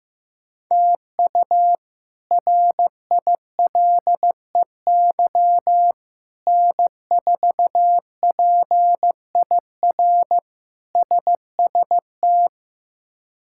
Morse code, heard as a keyed tone, about 15 wpm, 700 Hz. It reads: TU RILEY N4PIR SST
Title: K1USN SST QSO Practice - 15wpm